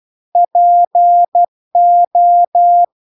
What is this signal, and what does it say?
Morse code, heard as a keyed tone, about 12 wpm, 700 Hz. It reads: PO